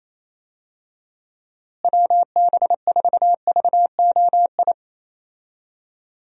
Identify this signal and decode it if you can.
Morse code, heard as a keyed tone, about 28 wpm, 700 Hz. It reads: WB4VOI
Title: USA Call Signs - 6 Characters - Repeated 3x - 28wpm